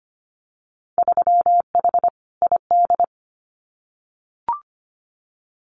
Morse code, heard as a keyed tone, about 25 wpm, 700 Hz. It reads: 3H ID
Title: Winter Field Day Exchanges Encore 25wpm